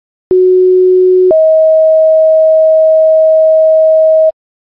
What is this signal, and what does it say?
Two-tone page, heard as a one-second tone, then a three-second tone, about 370 then 640 Hz. It is heard from 0.3 to 4.3 s.